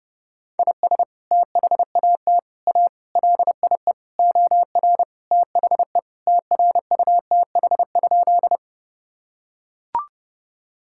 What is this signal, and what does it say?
Morse code, heard as a keyed tone, about 30 wpm, 700 Hz. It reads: IS THAT A LIE OR THE TRUTH?
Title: Sentences from Top 1600 Words 30wpm